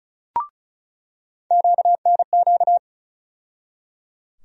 Morse code, heard as a keyed tone, about 34 wpm, 700 Hz. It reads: QNQ
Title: USA Call Sign Postfixes - 3 Letters 35wpm